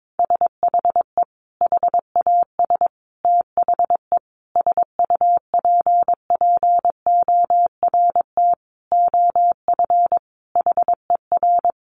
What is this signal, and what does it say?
Morse code, heard as a keyed tone, about 22 wpm, 700 Hz. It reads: SHE HAS THE SUPPORT OF HER
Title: Sentences from Top 800 Words 22wpm